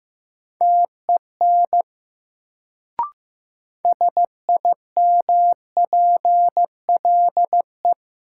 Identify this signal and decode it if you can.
Morse code, heard as a keyed tone, about 15 wpm, 700 Hz. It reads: TEN SIMPLE
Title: Sentences from Top 400 Words - New Word Drill - 15wpm